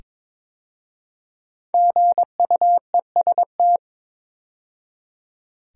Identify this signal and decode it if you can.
Morse code, heard as a keyed tone, about 22 wpm, 700 Hz. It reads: GUEST
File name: 5 Letter Words - First 3 Letters Given - 22wpm